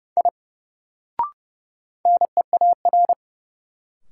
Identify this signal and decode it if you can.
Morse code, heard as a keyed tone, about 30 wpm, 700 Hz. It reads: I NEAR
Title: Sentences from Top 200 Words - New Word Drill - 30wpm